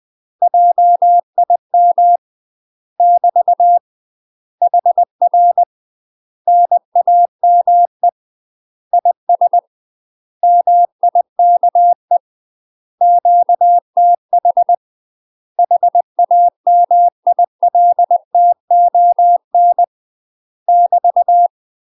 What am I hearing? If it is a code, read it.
Morse code, heard as a keyed tone, about 20 wpm, 700 Hz. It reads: JIM <BT> HR NAME IS MIKE QTH HAMILTON <BT>
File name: UK MASTER QSOs TRAINING/20WPM